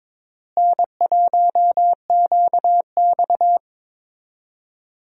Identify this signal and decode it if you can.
Morse code, heard as a keyed tone, about 21 wpm, 700 Hz. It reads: N1QX
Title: USA Call Signs - 4 Characters - Repeated 3x - 22wpm